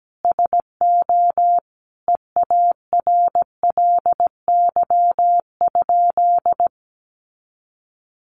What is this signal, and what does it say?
Morse code, heard as a keyed tone, about 17 wpm, 700 Hz. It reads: SO EARLY?
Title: Sentences from Top 400 Words 17wpm